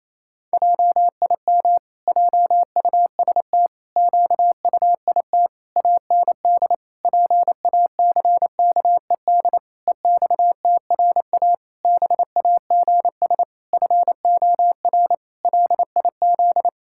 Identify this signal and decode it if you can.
Morse code, heard as a keyed tone, about 28 wpm, 700 Hz. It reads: JIM JUST QUIT AND PACKED EXTRA BAGS FOR LIZ
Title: Pangram Challenge v2 - 28wpm